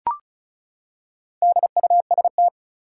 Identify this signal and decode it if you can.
Morse code, heard as a keyed tone, about 36 wpm, 700 Hz. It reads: DUST
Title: Four Letter Words - 35wpm